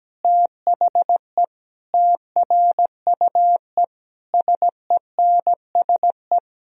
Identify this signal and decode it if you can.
Morse code, heard as a keyed tone, about 17 wpm, 700 Hz. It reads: THE TRUE SENSE
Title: Sentences from Top 1000 Words - Review Part 4 - 17wpm